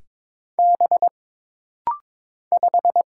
Morse code, heard as a keyed tone, about 22 wpm, 700 Hz. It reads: B 5
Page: This practice set is composed of Letters, Numbers, common punctuations, and common prosigns.